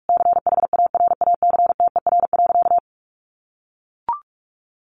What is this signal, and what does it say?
Morse code, heard as a keyed tone, about 44 wpm, 700 Hz. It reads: CHARACTER.
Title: Sentences from Top 1200 Words 45wpm